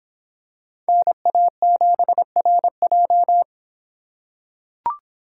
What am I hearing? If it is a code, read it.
Morse code, heard as a keyed tone, about 25 wpm, 700 Hz. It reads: NA7RJ